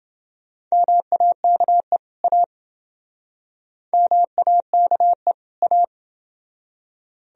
Morse code, heard as a keyed tone, about 28 wpm, 700 Hz. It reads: MAKE A MAKE A